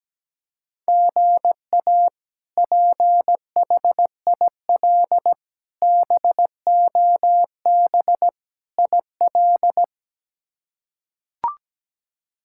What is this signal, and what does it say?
Morse code, heard as a keyed tone, about 17 wpm, 700 Hz. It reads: GA PHIL BOB IL
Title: K1USN SST QSO Practice - 17wpm